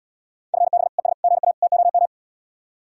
Morse code, heard as a keyed tone, about 72 wpm, 700 Hz. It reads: FB ES GA TOM